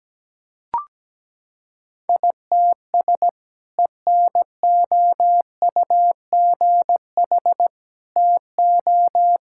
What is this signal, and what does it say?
Morse code, heard as a keyed tone, about 17 wpm, 700 Hz. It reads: ITS ENOUGH TO
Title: Sentences from Top 1000 Words - Review Part 6 - 17wpm